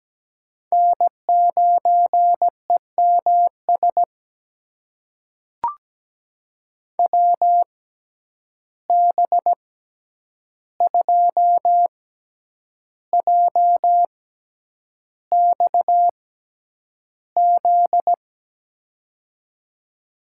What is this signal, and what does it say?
Morse code, heard as a keyed tone, about 17 wpm, spaced as for 8 wpm, 700 Hz. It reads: N9EMS WB2JXZ
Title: USA Call Signs - 6x Character Spacing - 17wpm